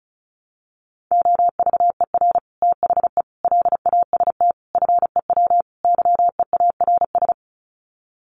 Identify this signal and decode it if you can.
Morse code, heard as a keyed tone, about 34 wpm, 700 Hz. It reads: OVER THE LAST FEW YEARS
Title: Sets of 5 Words Encore 35wpm